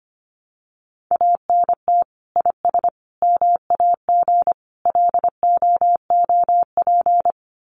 Morse code, heard as a keyed tone, about 25 wpm, 700 Hz. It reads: ANT IS MAG LOOP
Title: Common QSO Phrases 25wpm